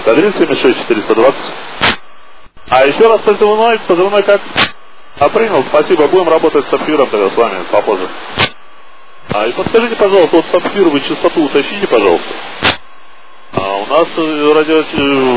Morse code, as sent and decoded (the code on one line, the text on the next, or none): none